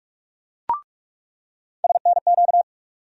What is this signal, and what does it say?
Morse code, heard as a keyed tone, about 46 wpm, 700 Hz. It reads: SNQ
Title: USA Call Sign Postfixes - 3 Letters 45wpm